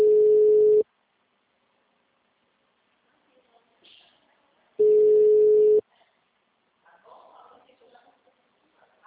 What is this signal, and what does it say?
Ringback tone (Europe, 425 Hz), with speech.
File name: out- -2017- - - .77862